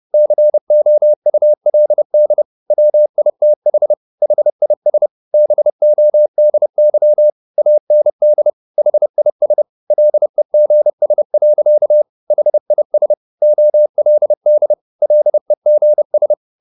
Morse code, heard as a keyed tone, about 30 wpm, 600 Hz. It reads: COULD WITH HIS BODY AND HIS LEGS. HIS OLD LEGS